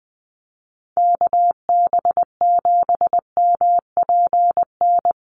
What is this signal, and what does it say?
Morse code, heard as a keyed tone, about 20 wpm, 700 Hz. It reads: KB7MPN
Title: USA Call Signs Repeated 3x 20wpm